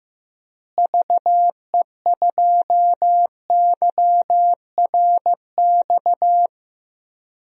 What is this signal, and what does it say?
Morse code, heard as a keyed tone, about 15 wpm, 700 Hz. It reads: VE2YRX
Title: DXCC-100 Call Signs - 6 Characters - Repeated 3x - 15wpm